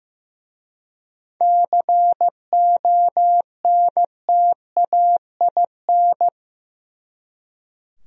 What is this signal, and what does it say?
Morse code, heard as a keyed tone, about 18 wpm, 700 Hz. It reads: CONTAIN
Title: Speed-Racing Top 500 Words 15wpm